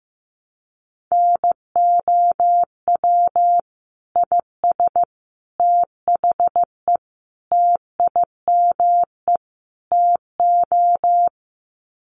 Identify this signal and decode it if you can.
Morse code, heard as a keyed tone, about 15 wpm, 700 Hz. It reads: NOW IS THE TIME TO
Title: Sets of 5 Words Encore2 15wpm